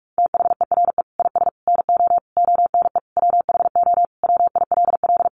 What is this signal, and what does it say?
Morse code, heard as a keyed tone, about 44 wpm, 700 Hz. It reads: THERE IS NO ONE WHO WILL
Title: Sentences from Top 1000 Words 45wpm